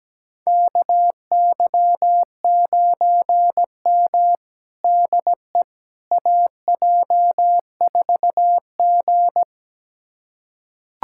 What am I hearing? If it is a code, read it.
Morse code, heard as a keyed tone, about 17 wpm, 700 Hz. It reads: KY9M DE AJ4G
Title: USA Callsign DE Callsign - 4 Characters - 17wpm